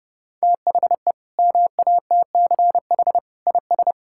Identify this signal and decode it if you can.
Morse code, heard as a keyed tone, about 32 wpm, 700 Hz. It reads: THE MATCH IS
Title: Sentences from Top 1600 Words 30wpm